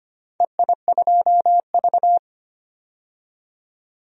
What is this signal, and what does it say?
Morse code, heard as a keyed tone, about 25 wpm, 700 Hz. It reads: EI2V
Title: DXCC-100 Call Signs - 4 Characters - 25wpm